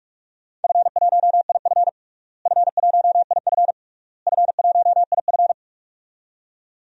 Morse code, heard as a keyed tone, about 46 wpm, 700 Hz. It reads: F1IF F1IF F1IF
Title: DXCC-100 Call Signs - 4 Characters - Repeated 3x - 45wpm